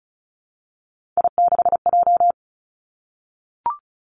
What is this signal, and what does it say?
Morse code, heard as a keyed tone, about 36 wpm, 700 Hz. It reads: I6J
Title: Three Challenging Letter-Number Combinations 35wpm